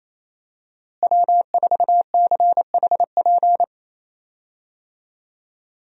Morse code, heard as a keyed tone, about 28 wpm, 700 Hz. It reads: W4CHP
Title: USA Call Signs - 5 Characters - Repeated 2x - 28wpm